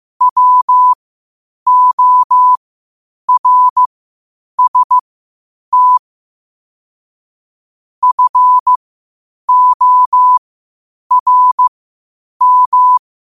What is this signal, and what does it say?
Morse code, heard as a keyed tone, about 15 wpm, spaced as for 8 wpm, 1000 Hz. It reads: WORST FORM
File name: qotdfiles / 8wpm